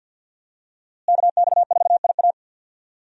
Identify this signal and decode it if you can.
Morse code, heard as a keyed tone, about 51 wpm, 700 Hz. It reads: KX4IA
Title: USA Call Signs - 5 Characters - Repeated 3x - 50wpm